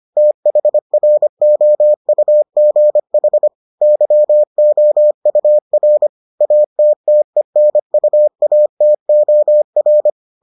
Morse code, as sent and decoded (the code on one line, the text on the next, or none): - .... .-. --- ..- --. .... / -.-- --- ..- .-. / .- - - . -. ..- .- - --- .-.
THROUGH YOUR ATTENUATOR